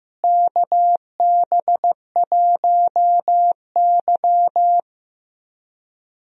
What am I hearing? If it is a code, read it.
Morse code, heard as a keyed tone, about 16 wpm, 700 Hz. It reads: KB1Y
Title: USA Call Signs - 4 Characters - Repeated 3x - 15wpm